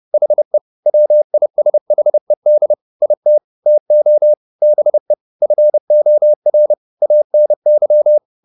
Morse code, heard as a keyed tone, about 30 wpm, 600 Hz. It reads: HE WISHED IT TO BE FOR ANY
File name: the old man and the sea / the old man and the sea 1 30wpm